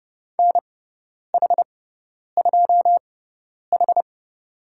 Morse code, heard as a keyed tone, about 30 wpm, spaced as for 10 wpm, 700 Hz. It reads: NH2H
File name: Lesson 18g - Letter m - Call Signs 30wpm